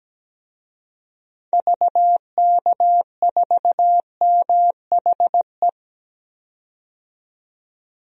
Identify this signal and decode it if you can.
Morse code, heard as a keyed tone, about 17 wpm, 700 Hz. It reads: VK4MHE
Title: DXCC-100 Call Signs - 6 Characters - 17wpm